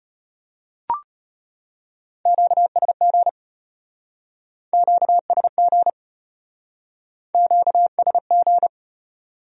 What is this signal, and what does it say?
Morse code, heard as a keyed tone, about 34 wpm, 700 Hz. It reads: QSG QSG QSG